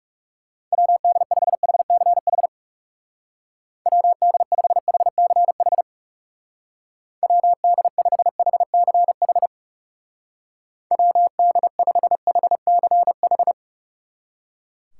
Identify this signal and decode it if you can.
Morse code, heard as a keyed tone, about 35 wpm, 700 Hz. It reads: WD5HCH WD5HCH WD5HCH WD5HCH
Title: Speed Racing - USA 2x3 Call Signs 30wpm